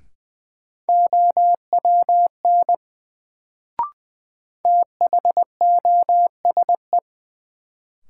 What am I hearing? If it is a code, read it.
Morse code, heard as a keyed tone, about 20 wpm, 700 Hz. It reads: OWN THOSE
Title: Sentences from Top 300 Words - New Word Drill - 20wpm